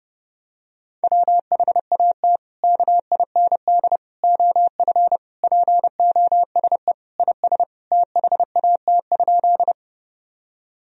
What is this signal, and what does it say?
Morse code, heard as a keyed tone, about 30 wpm, 700 Hz. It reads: WHAT KIND OF POSE IS THAT?
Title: Sentences from Top 1000 Words - Review Part 2 - 30wpm